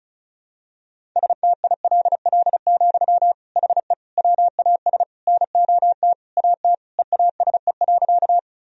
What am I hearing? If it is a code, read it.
Morse code, heard as a keyed tone, about 34 wpm, 700 Hz. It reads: STILL, HE WAS NOT AT EASE.
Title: Sentences from Top 1000 Words - Review Part 1 - 35wpm